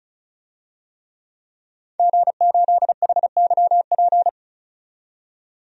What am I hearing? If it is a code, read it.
Morse code, heard as a keyed tone, about 36 wpm, 700 Hz. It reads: G8HYP
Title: DXCC-100 Call Signs - 5 Characters - 35wpm